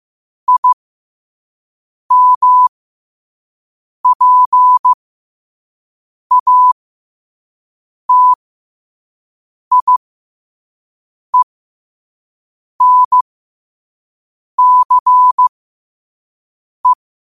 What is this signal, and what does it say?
Morse code, heard as a keyed tone, about 15 wpm, spaced as for 5 wpm, 1000 Hz. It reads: IMPATIENCE
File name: 5wpm